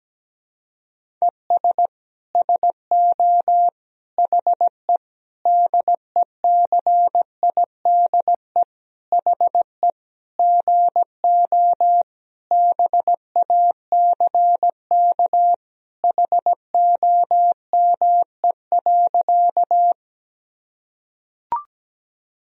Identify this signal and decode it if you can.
Morse code, heard as a keyed tone, about 17 wpm, 700 Hz. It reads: ES SO HE DECIDE HE GO BACK HOME.
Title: Sentences from Top 500 Words and Abbreviations 17wpm